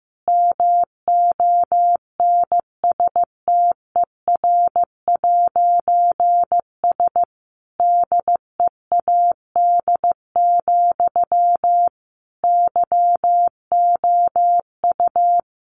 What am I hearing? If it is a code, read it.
Morse code, heard as a keyed tone, about 15 wpm, 700 Hz. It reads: MONSTER'S DEAD, YOU
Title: A Sound Of Thunder 15wpm